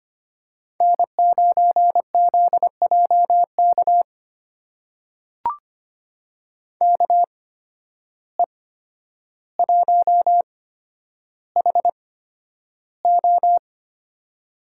Morse code, heard as a keyed tone, about 25 wpm, spaced as for 11 wpm, 700 Hz. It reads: N9ZJK KE1HO
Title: USA Call Signs - 5 Characters - 8x Character Spacing - 25wpm